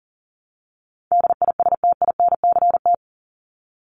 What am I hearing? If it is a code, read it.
Morse code, heard as a keyed tone, about 40 wpm, 700 Hz. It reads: DISTINCT
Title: Top 1501-1750 Words 40wpm